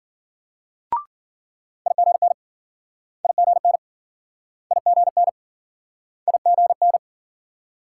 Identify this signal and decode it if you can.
Morse code, heard as a keyed tone, about 52 wpm, 700 Hz. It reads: IGN IGN IGN IGN